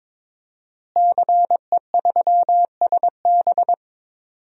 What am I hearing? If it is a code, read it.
Morse code, heard as a keyed tone, about 22 wpm, 700 Hz. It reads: CE3SB
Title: DXCC-100 Call Signs - 5 Characters - Repeated 2x - 22wpm